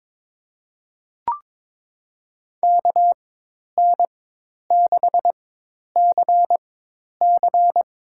Morse code, heard as a keyed tone, about 21 wpm, spaced as for 10 wpm, 700 Hz. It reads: KN6CC
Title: USA Call Signs - 5 Characters - 4x Character Spacing - 22wpm